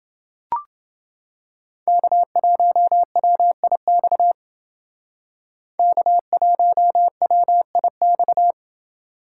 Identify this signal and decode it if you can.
Morse code, heard as a keyed tone, about 28 wpm, 700 Hz. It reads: K1WIX K1WIX